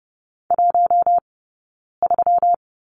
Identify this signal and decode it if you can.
Morse code, heard as a keyed tone, about 30 wpm, 700 Hz. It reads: 1 3
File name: Lesson 20d - Number 3 - Two Characters 30wpm